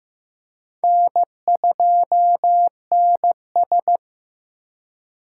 Morse code, heard as a keyed tone, about 15 wpm, 700 Hz. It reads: N2NS
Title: K1USN SST QSO Practice - 15wpm